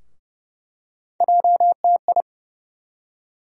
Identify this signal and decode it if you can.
Morse code, heard as a keyed tone, about 30 wpm, 700 Hz. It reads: JTI
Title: USA Call Sign Postfixes - 3 Letters 30wpm